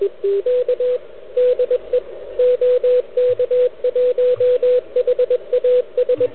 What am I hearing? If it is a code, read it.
Morse code, heard as a keyed tone, about 22 wpm, 500 Hz, with radio noise. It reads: K DE OK1HAS
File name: ARCHIV / QCX / NedTest